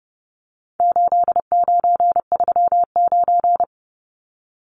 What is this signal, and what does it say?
Morse code, heard as a keyed tone, about 30 wpm, 700 Hz. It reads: 8939
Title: Numbers - 4-Digit 30wpm